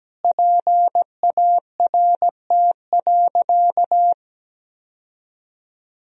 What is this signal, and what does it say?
Morse code, heard as a keyed tone, about 17 wpm, 700 Hz. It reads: PART.
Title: Sentences from Top 1000 Words - Review Part 1 - 17wpm